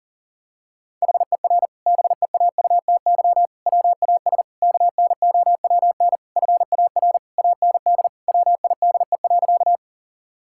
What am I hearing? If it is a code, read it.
Morse code, heard as a keyed tone, about 40 wpm, 690 Hz. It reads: HER BEAUTY WAS KNOWN FAR AND WIDE.
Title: Sentences from Top 1000 Words - Review Part 6 - 40wpm